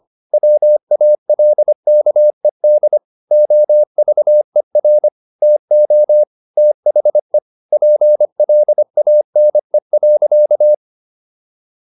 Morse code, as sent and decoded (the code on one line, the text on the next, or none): .-- .- .-.. -.- . -.. / --- ...- . .-. / - --- / - .... / .--. .-.. .- -. .-.-.-
WALKED OVER TO THE PLANE.